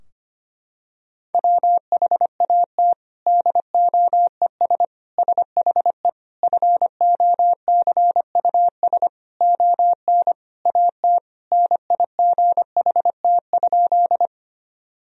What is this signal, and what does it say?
Morse code, heard as a keyed tone, about 25 wpm, 700 Hz. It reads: WHAT DOES SHE FOCUS ON AT NIGHT?
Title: Sentences from Top 1000 Words - Review Part 4 - 25wpm